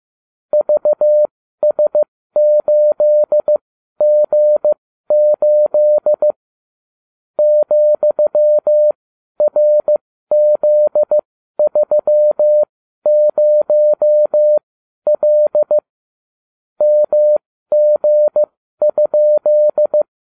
Morse code, heard as a keyed tone, about 15 wpm, spaced as for 12 wpm, 600 Hz. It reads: VS8G8 ,RZ30L MG?